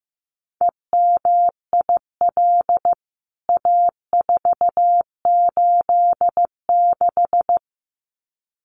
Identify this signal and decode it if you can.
Morse code, heard as a keyed tone, about 15 wpm, 700 Hz. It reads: EMIL A486
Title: CWT Exchange 15wpm